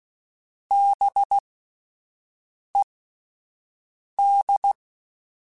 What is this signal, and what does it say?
Morse code, heard as a keyed tone, about 16 wpm, spaced as for 5 wpm, 800 Hz. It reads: BED